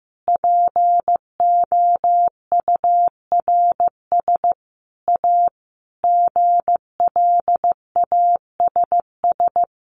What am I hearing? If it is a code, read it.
Morse code, heard as a keyed tone, about 15 wpm, 700 Hz. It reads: POURS A GLASS
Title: Additional Easy Phrases 15wpm